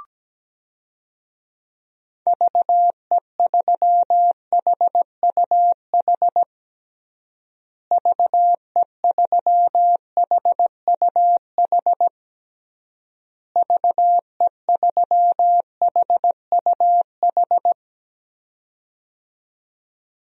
Morse code, heard as a keyed tone, about 17 wpm, 700 Hz. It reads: VE3HUH VE3HUH VE3HUH
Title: DXCC-100 Call Signs - 6 Characters - Repeated 3x - 17wpm